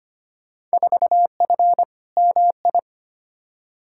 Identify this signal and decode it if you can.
Morse code, heard as a keyed tone, about 25 wpm, 700 Hz. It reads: 4F MI
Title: Field Day Exchanges - 25wpm